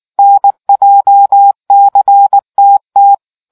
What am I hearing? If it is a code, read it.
Morse code, heard as a keyed tone, about 19 wpm, 800 Hz. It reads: NJCTT